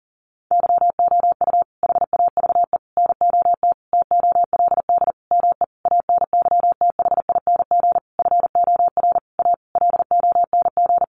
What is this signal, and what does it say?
Morse code, heard as a keyed tone, about 40 wpm, 700 Hz. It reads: YOU HAVE NOT TOLD ME ANYTHING FOR A LONG
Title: Sentences from Top 1000 Words 40wpm